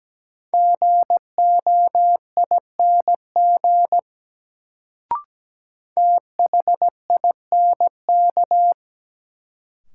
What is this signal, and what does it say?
Morse code, heard as a keyed tone, about 17 wpm, 700 Hz. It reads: GOING THINK
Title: Sets of 2 Words - New Word Drill - 17wpm